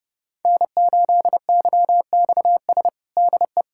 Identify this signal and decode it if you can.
Morse code, heard as a keyed tone, about 32 wpm, 700 Hz. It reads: N8YXS DE
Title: USA Callsign DE Callsign 30wpm